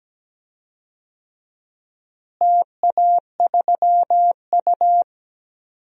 Morse code, heard as a keyed tone, about 17 wpm, 700 Hz. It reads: TA3U
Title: DXCC-100 Call Signs - 4 Characters - Repeated 2x - 17wpm